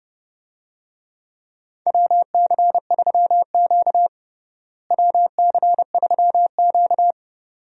Morse code, heard as a keyed tone, about 32 wpm, 700 Hz. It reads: WC3Q WC3Q
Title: USA Call Signs - 4 Characters - Repeated 2x - 30wpm